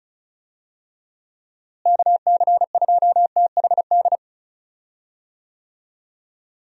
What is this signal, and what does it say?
Morse code, heard as a keyed tone, about 36 wpm, 700 Hz. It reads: KC2THD
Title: USA Call Signs - 6 Characters - Repeated 3x - 35wpm